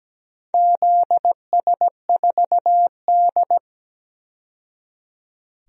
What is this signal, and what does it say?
Morse code, heard as a keyed tone, about 17 wpm, 700 Hz. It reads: ZS4D
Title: DXCC-100 Call Signs - 4 Characters - 17wpm